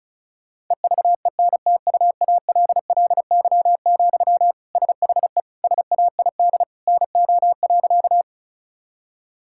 Morse code, heard as a keyed tone, about 34 wpm, 700 Hz. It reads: EVENTUALLY, SHE SAID NO.
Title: Sentences from Top 1000 Words - Review Part 6 - 35wpm